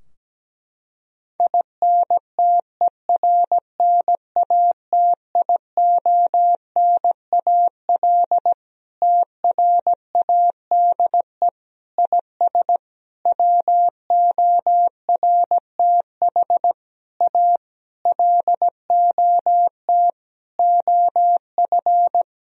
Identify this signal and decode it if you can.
Morse code, heard as a keyed tone, about 17 wpm, 700 Hz. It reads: INTERNATIONAL TRADE IS WORTH A LOT OF
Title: Sentences from Top 1000 Words - Review Part 4 - 17wpm